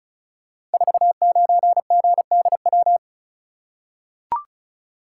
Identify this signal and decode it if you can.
Morse code, heard as a keyed tone, about 36 wpm, 690 Hz. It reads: 49GDW
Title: Code Groups - 5 Letter:Numbers - 35wpm